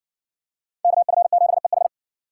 Morse code, heard as a keyed tone, about 56 wpm, 700 Hz. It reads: KF7EH